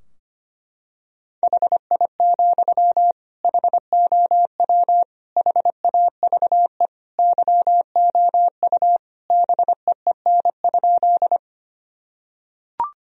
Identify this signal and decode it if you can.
Morse code, heard as a keyed tone, about 25 wpm, 700 Hz. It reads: HI, HOW HAVE YOU BEEN?
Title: Sentences from Top 1000 Words - Review Part 3 - 25wpm